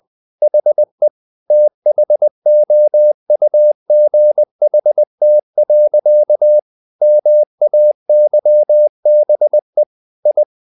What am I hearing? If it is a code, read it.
Morse code, heard as a keyed tone, about 20 wpm, 600 Hz. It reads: HE THOUGHT. MAYBE I
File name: the old man and the sea part 2 20wpm